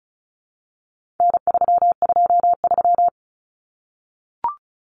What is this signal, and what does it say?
Morse code, heard as a keyed tone, about 34 wpm, 700 Hz. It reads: N323
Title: Common Cut Numbers - 4-Digit 35wpm